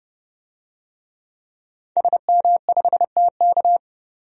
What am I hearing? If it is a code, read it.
Morse code, heard as a keyed tone, about 30 wpm, 700 Hz. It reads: SM5TK
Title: DXCC-100 Call Signs - 5 Characters - 30wpm